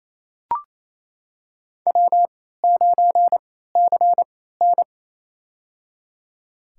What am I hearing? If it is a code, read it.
Morse code, heard as a keyed tone, about 28 wpm, spaced as for 16 wpm, 700 Hz. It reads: W9CN
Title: USA Call Signs - 4 Characters - 3x Character Spacing - 28wpm